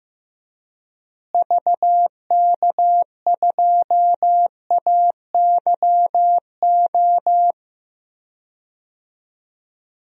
Morse code, heard as a keyed tone, about 15 wpm, 700 Hz. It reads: VK2AYO
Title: DXCC-100 Call Signs - 6 Characters - Repeated 3x - 15wpm